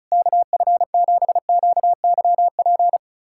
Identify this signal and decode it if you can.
Morse code, heard as a keyed tone, about 36 wpm, 700 Hz. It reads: KF7QYP
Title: USA Callsign DE Callsign - 6 Characters - 35wpm